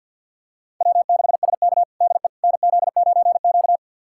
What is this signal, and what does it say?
Morse code, heard as a keyed tone, about 51 wpm, 700 Hz. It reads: W6SK DE NZ9Q
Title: USA Callsign DE Callsign - 4 Characters - 50wpm